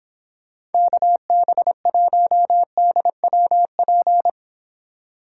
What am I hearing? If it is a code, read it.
Morse code, heard as a keyed tone, about 29 wpm, 700 Hz. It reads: KB1DWP